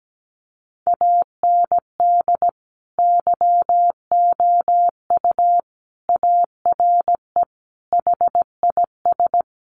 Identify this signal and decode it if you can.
Morse code, heard as a keyed tone, about 17 wpm, 700 Hz. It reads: AND YOU ARE HIS
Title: Sentences from Top 1300 Words 17wpm